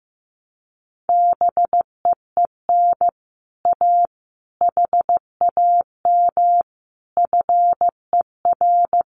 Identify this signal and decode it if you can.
Morse code, heard as a keyed tone, about 15 wpm, 700 Hz. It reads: BEEN A HAM FER
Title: Common QSO Phrases Encore 15wpm